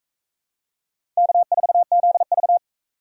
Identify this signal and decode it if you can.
Morse code, heard as a keyed tone, about 38 wpm, 700 Hz. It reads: K4ZV